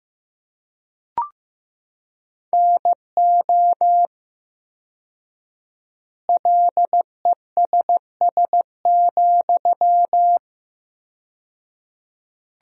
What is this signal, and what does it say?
Morse code, heard as a keyed tone, about 15 wpm, 700 Hz. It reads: NO LESS,
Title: 4x Word-Spacing - Sentences from Top 400 Words - 15wpm